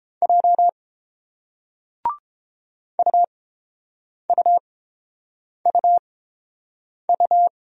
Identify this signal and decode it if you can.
Morse code, heard as a keyed tone, about 30 wpm, spaced as for 7 wpm, 700 Hz. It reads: J UUUU